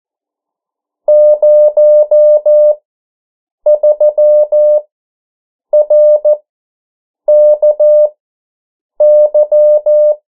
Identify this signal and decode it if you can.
Morse code, heard as a keyed tone, about 14 wpm, spaced as for 7 wpm, 600 Hz. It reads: 03RKY
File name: morse mp3 / 09 urok 8